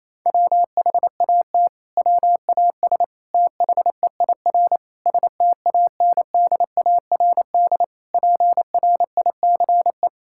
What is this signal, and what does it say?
Morse code, heard as a keyed tone, about 28 wpm, 700 Hz. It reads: WHAT WAS THEIR STANDARD PRICE
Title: Sentences from Top 1000 Words - Review Part 6 - 28wpm